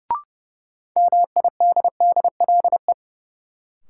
Morse code, heard as a keyed tone, about 30 wpm, 700 Hz. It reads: MIDDLE
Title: Sets of 2 Words - New Word Drill - 30wpm